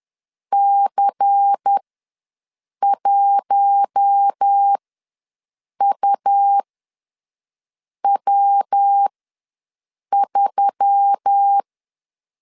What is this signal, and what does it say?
Morse code, heard as a keyed tone, about 10 wpm, spaced as for 6 wpm, 800 Hz. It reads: C1UW3
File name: LEZIONI / 5 LBVC / 5 LBVC ws 10wpm 3733